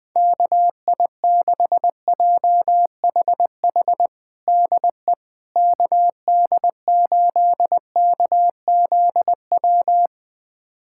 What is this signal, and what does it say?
Morse code, heard as a keyed tone, about 21 wpm, 700 Hz. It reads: KI6JHH DE KD8KZW
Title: USA Callsign DE Callsign 20wpm